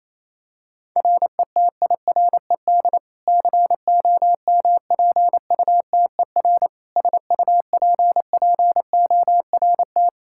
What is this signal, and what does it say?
Morse code, heard as a keyed tone, about 28 wpm, 700 Hz. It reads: RETIRED COMPUTER SUPPORT
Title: Practice QSO - Exchange 3 - More Info - 28wpm